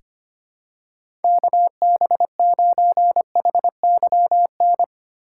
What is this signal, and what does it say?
Morse code, heard as a keyed tone, about 25 wpm, 700 Hz. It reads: KB9HYN
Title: USA Call Signs - 2x Character Spacing - 25wpm